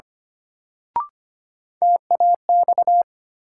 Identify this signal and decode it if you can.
Morse code, heard as a keyed tone, about 25 wpm, 700 Hz. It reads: TAX
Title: Sentences from Top 1000 Words - Review Part 5 - New Word Drill - 25wpm